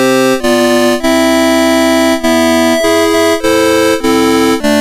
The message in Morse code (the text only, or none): none